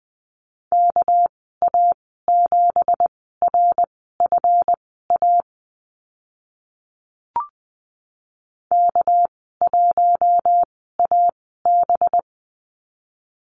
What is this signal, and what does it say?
Morse code, heard as a keyed tone, about 20 wpm, spaced as for 14 wpm, 700 Hz. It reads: KA7RFA K1AB
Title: 207 Lesson 33g - Letter j - Call Signs 20wpm